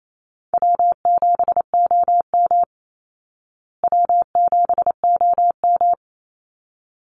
Morse code, heard as a keyed tone, about 28 wpm, 690 Hz. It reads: W7OM W7OM
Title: Call Signs Repeated 3x - 28wpm